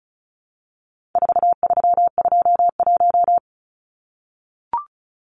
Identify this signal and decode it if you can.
Morse code, heard as a keyed tone, about 34 wpm, 700 Hz. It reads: 4321
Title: Numbers - 4-Digit 35wpm